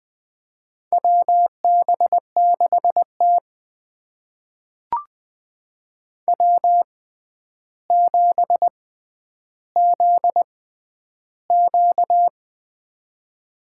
Morse code, heard as a keyed tone, about 20 wpm, spaced as for 10 wpm, 700 Hz. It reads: WB6T W7ZQ
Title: USA Call Signs - 4 Characters - 6x Character Spacing - 20wpm